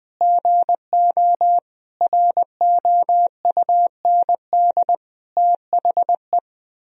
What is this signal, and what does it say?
Morse code, heard as a keyed tone, about 20 wpm, 700 Hz. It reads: GO ROUND THE
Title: Sentences from Top 1000 Words - Review Part 4 - 20wpm